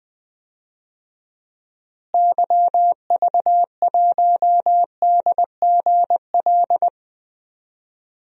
Morse code, heard as a keyed tone, about 20 wpm, 700 Hz. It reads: YV1DGL
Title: DXCC-100 Call Signs - 6 Characters - Repeated 3x - 20wpm